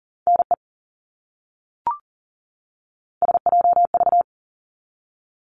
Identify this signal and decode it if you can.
Morse code, heard as a keyed tone, about 40 wpm, 700 Hz. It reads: NE SJV
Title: ARRL Sections 40wpm